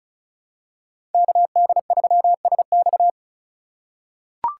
Morse code, heard as a keyed tone, about 36 wpm, 700 Hz. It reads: KD3SX
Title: USA Call Signs - 5 Characters - 4x Character Spacing - 35wpm